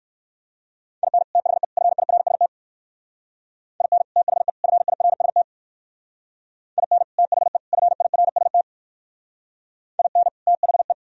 Morse code, heard as a keyed tone, about 57 wpm, 700 Hz. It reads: IN THE FIRST IN THE FIRST IN THE FIRST IN THE